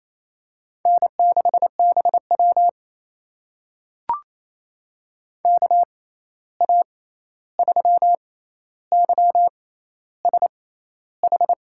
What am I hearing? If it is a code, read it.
Morse code, heard as a keyed tone, about 28 wpm, spaced as for 13 wpm, 700 Hz. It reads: N6BW KA3YSH
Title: USA Call Signs - 6x Character Spacing - 28wpm